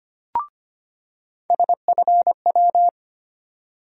Morse code, heard as a keyed tone, about 25 wpm, 700 Hz. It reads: SFW
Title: USA Call Sign Postfixes - 3 Letters 25wpm